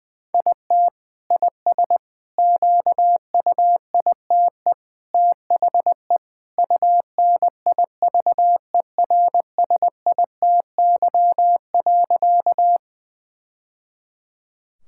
Morse code, heard as a keyed tone, about 20 wpm, 700 Hz. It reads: IT IS QUITE THE UNIVERSITY.
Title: Sentences from Top 1000 Words - Review Part 5 - 20wpm